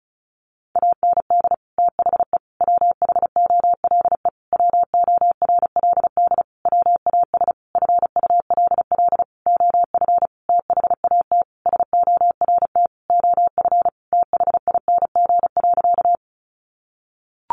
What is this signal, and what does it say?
Morse code, heard as a keyed tone, about 34 wpm, 700 Hz. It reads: AND THE WHOLE WORLD WAS FULL OF THAT SORT OF THING.
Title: Sentences from Top 1300 Words 35wpm